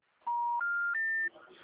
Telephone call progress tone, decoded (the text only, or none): special information tone